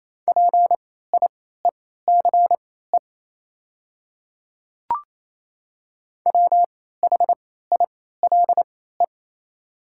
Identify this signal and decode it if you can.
Morse code, heard as a keyed tone, about 27 wpm, spaced as for 16 wpm, 700 Hz. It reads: PIECE WHILE
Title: 3x Character Spacing - Top 500 Words - 28wpm